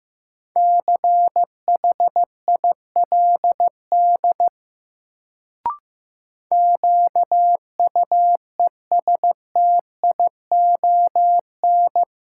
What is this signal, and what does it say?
Morse code, heard as a keyed tone, about 15 wpm, 700 Hz. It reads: CHILD QUESTION
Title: Sets of 5 Words Encore - New Word Drill - 15wpm